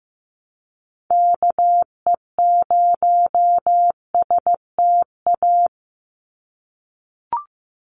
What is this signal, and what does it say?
Morse code, heard as a keyed tone, about 15 wpm, 700 Hz. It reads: KE0STA
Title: USA Call Signs 15wpm